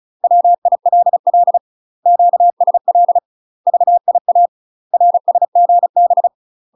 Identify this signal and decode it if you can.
Morse code, heard as a keyed tone, about 36 wpm, 700 Hz. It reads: WILL QSL VIA RSGB
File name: UK MASTER QSOs TRAINING / 35WPM / UK MASTER QSOs 0007